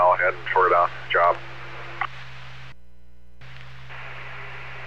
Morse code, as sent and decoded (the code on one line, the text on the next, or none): none